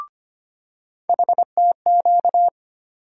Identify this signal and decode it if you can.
Morse code, heard as a keyed tone, about 25 wpm, 700 Hz. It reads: HTQ
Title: USA Call Sign Postfixes - 3 Letters 25wpm